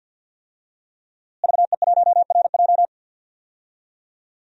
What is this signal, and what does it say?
Morse code, heard as a keyed tone, about 51 wpm, 700 Hz. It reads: VE1RJ
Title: DXCC-100 Call Signs - 5 Characters - Repeated 3x - 50wpm